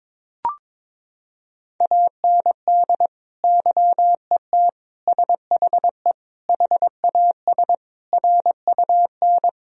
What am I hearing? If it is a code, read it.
Morse code, heard as a keyed tone, about 22 wpm, 700 Hz. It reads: AND YET SHE HAS RUN
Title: Sentences from Top 1000 Words - Review Part 1 - 22wpm